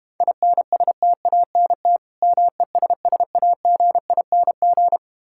Morse code, heard as a keyed tone, about 31 wpm, 700 Hz. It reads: INSTANT MESSAGING